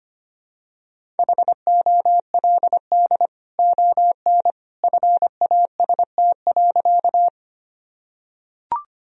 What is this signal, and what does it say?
Morse code, heard as a keyed tone, about 25 wpm, 700 Hz. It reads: HOLD ON FAST.
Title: Sentences from Top 1000 Words - Review Part 2 - 25wpm